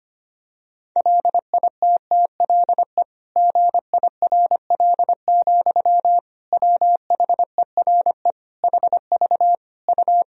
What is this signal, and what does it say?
Morse code, heard as a keyed tone, about 25 wpm, 700 Hz. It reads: LITTLE GIRL, WHERE HV U
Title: Sentences from Top 400 Words and Abbreviations 25wpm